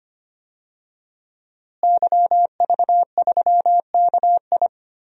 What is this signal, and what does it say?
Morse code, heard as a keyed tone, about 25 wpm, 700 Hz. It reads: YV3KI
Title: DXCC-100 Call Signs - 5 Characters - Repeated 2x - 25wpm